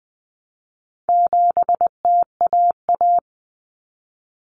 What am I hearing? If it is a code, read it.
Morse code, heard as a keyed tone, about 20 wpm, 700 Hz. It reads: 7TAA
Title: Common Cut Numbers - 4-Digit 20wpm